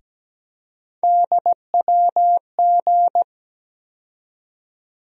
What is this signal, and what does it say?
Morse code, heard as a keyed tone, about 17 wpm, 700 Hz. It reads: DWG
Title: USA Call Sign Postfixes - 3 Letters 17wpm